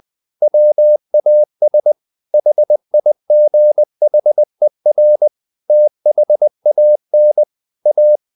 Morse code, heard as a keyed tone, about 20 wpm, 600 Hz. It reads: WAS HIGHER THAN A
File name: the old man and the sea part 2 20wpm